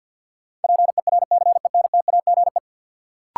Morse code, heard as a keyed tone, about 52 wpm, 700 Hz. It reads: PERCENTAGE
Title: Sentences from Top 1300 Words - New Word Drill - 50wpm